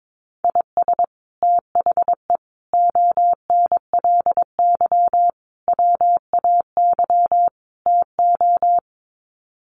Morse code, heard as a keyed tone, about 22 wpm, 700 Hz. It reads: IS THE ONLY WAY TO
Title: Sets of 5 Words Encore 22wpm